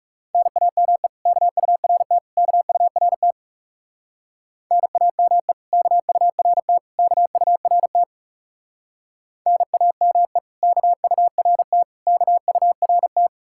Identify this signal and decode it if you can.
Morse code, heard as a keyed tone, about 36 wpm, 700 Hz. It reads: NAME KURT KURT NAME KURT KURT NAME KURT KURT
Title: Speed-Racing Common QSO Phrases - 30wpm